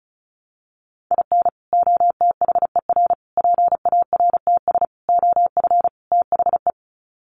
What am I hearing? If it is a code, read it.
Morse code, heard as a keyed tone, about 34 wpm, 700 Hz. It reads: IN OTHER PARTS OF THE
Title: Sets of 5 Words 35wpm